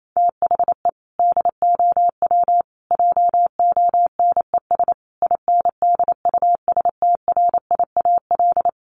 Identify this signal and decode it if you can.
Morse code, heard as a keyed tone, about 28 wpm, 700 Hz. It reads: THE DOW JONES INDUSTRIAL
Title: Sets of 5 Words 28wpm